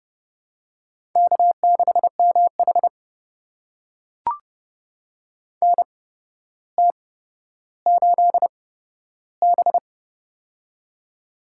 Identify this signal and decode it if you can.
Morse code, heard as a keyed tone, about 30 wpm, spaced as for 13 wpm, 700 Hz. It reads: K6MH NT8B
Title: USA Call Signs - 4 Characters - 8x Character Spacing - 30wpm